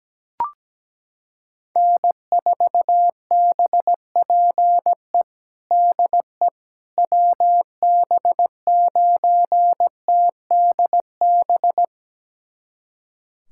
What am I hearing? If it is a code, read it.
Morse code, heard as a keyed tone, about 17 wpm, 700 Hz. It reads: N4BPE DE WB9TDB
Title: USA Callsign DE Callsign 17wpm